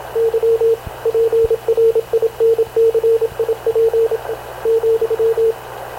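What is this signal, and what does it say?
Morse code, heard as a keyed tone, about 26 wpm, 460 Hz, with radio noise. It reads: Y PRINCIP ,
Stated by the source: Havana Radio CLA press (PX) broadcast, recorded by Radio Officer Paul Dumesnil aboard mv CNAV QUEST/CZDO.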